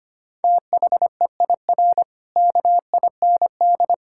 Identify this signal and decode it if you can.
Morse code, heard as a keyed tone, about 25 wpm, 700 Hz. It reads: THEIR KIND
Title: Pangram Challenge v2 - 25wpm